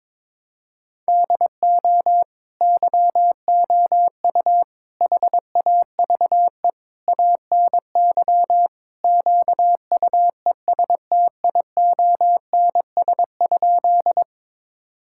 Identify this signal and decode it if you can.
Morse code, heard as a keyed tone, about 22 wpm, 700 Hz. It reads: DO YOU HAVE ANY QUESTIONS?
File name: practice / 22wpm / Sentences with 4-Word Phrases - 22wpm